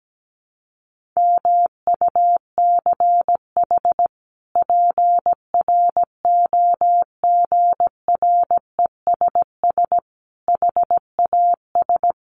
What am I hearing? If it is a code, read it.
Morse code, heard as a keyed tone, about 17 wpm, 700 Hz. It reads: MUCH PROGRESS HAS
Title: Sentences from Top 1400 Words 17wpm